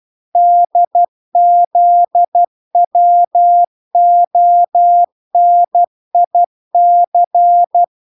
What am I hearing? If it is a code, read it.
Morse code, heard as a keyed tone, about 12 wpm, 700 Hz. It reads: DZWONIC